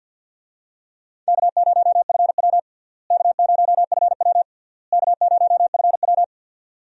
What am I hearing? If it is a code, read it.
Morse code, heard as a keyed tone, about 52 wpm, 700 Hz. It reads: K0FW K0FW K0FW
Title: USA Call Signs - 4 Characters - Repeated 3x - 50wpm